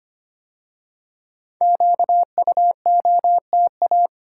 Morse code, heard as a keyed tone, about 25 wpm, 700 Hz. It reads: QUOTA
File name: 5 Letter Words - First 3 Letters Given - 25wpm